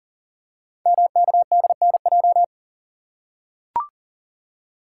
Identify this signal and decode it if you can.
Morse code, heard as a keyed tone, about 40 wpm, 690 Hz. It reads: MKDNJ
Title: Code Groups - 5 Letters - 40wpm